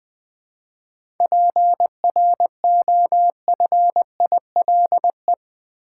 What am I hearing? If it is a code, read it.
Morse code, heard as a keyed tone, about 20 wpm, 700 Hz. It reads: PROFILE
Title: Seven Letter Words - 20wpm